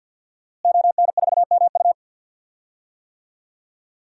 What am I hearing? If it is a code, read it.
Morse code, heard as a keyed tone, about 52 wpm, 700 Hz. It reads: ON4MU
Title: DXCC-100 Call Signs - 5 Characters - Repeated 3x - 50wpm